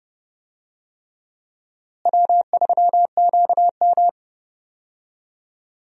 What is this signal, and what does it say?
Morse code, heard as a keyed tone, about 32 wpm, 700 Hz. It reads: W3QM
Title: USA Call Signs - 4 Characters - Repeated 3x - 30wpm